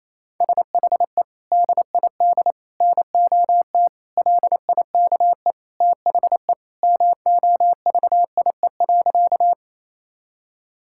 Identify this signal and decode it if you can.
Morse code, heard as a keyed tone, about 28 wpm, 700 Hz. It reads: SHE DID NOT LIKE THE MOVIE.
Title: Sentences from Top 1000 Words - Review Part 5 - 28wpm